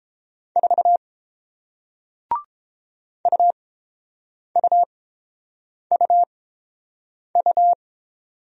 Morse code, heard as a keyed tone, about 30 wpm, spaced as for 7 wpm, 700 Hz. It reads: 4 UUUU